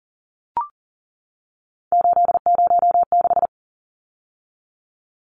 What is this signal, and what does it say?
Morse code, heard as a keyed tone, about 40 wpm, 700 Hz. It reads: 806
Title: Numbers - 3-Digit 40wpm